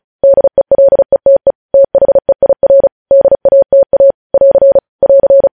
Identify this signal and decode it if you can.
Morse code, heard as a keyed tone, about 36 wpm, 550 Hz. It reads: DELETE THEIR DATA <AR> <AR>
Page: News Headlines in Morse Code at 35 WPM